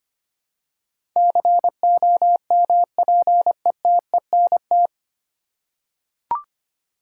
Morse code, heard as a keyed tone, about 25 wpm, 700 Hz. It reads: COMPETENT
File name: practice / 25wpm / 9 Letter Words - First 3 Letters Given - 25wpm